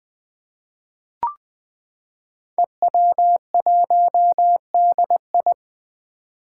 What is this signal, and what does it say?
Morse code, heard as a keyed tone, about 20 wpm, 700 Hz. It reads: EW1DI
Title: Five Challenging Letter-Number Combinations 20wpm